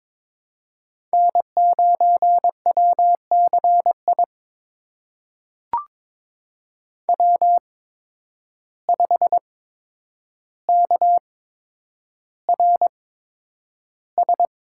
Morse code, heard as a keyed tone, about 22 wpm, spaced as for 9 wpm, 700 Hz. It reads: N9WCI W5KRS
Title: USA Call Signs - 5 Characters - 8x Character Spacing - 22wpm